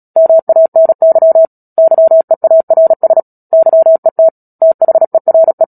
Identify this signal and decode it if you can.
Morse code, heard as a keyed tone, about 37 wpm, 650 Hz. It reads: MANY YEARS YET THERE
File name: Book Two / 10-Chapter 10 - Epilogue - 36 WPM